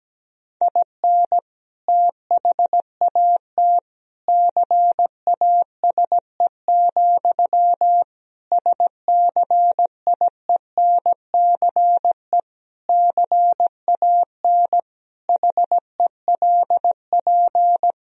Morse code, heard as a keyed tone, about 17 wpm, 700 Hz. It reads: IN THAT CASE, SCIENCE CAN HELP
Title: Sentences from Top 1000 Words - Review Part 2 - 17wpm